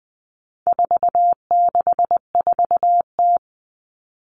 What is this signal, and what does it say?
Morse code, heard as a keyed tone, about 20 wpm, 700 Hz. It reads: 464T
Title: All Cut Numbers - 4-Digit 20wpm